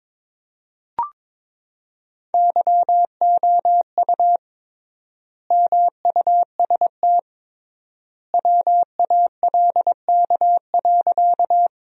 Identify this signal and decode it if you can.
Morse code, heard as a keyed tone, about 22 wpm, 700 Hz. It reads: YOU MUST WALK.
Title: 3x Word-Spacing - Sentences from Top 300 Words - 22wpm